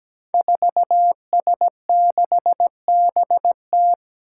Morse code, heard as a keyed tone, about 17 wpm, 700 Hz. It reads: 4S6BT
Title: Five Challenging Letter-Number Combinations 17wpm